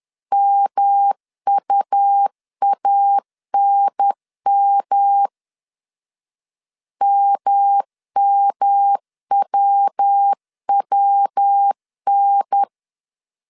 Morse code, heard as a keyed tone, about 10 wpm, 800 Hz. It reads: MUANM MMWWN
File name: CW / DET / 1 3 1500C S B 10wpm 3335